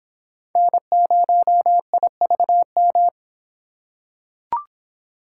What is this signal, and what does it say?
Morse code, heard as a keyed tone, about 26 wpm, 700 Hz. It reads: N0IVM